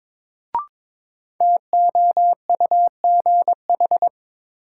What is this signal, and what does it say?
Morse code, heard as a keyed tone, about 21 wpm, 700 Hz. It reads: TOUGH
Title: Sentences from Top 1200 Words - New Word Drill Alt - 22wpm